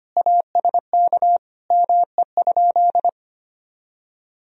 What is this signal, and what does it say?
Morse code, heard as a keyed tone, about 25 wpm, 700 Hz. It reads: ASK ME?
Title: Sentences from Top 200 Words and Abbreviations 25wpm